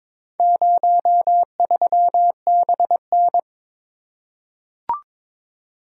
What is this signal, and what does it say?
Morse code, heard as a keyed tone, about 22 wpm, 700 Hz. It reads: 03BN
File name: practice / 22wpm / Four Letter-Number Combinations - 22wpm